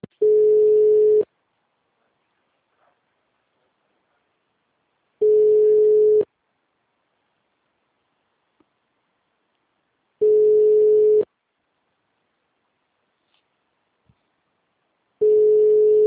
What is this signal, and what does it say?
Ringback tone (Europe, 425 Hz).